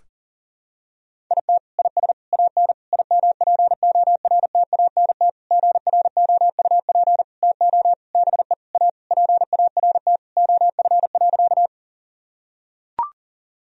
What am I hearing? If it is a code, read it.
Morse code, heard as a keyed tone, about 40 wpm, 700 Hz. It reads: IT IS AN IMPORTANT GROUP TO BE A PART OF.
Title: Sentences from Top 1000 Words - Review Part 1 - 40wpm